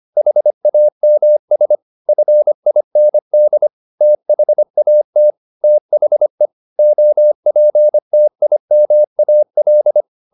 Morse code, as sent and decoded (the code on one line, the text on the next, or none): .... .- -- ... / ..-. .. -. -.. / - .... .- - / - .... / --- .--. - .. -- .- .-..
HAMS FIND THAT THE OPTIMAL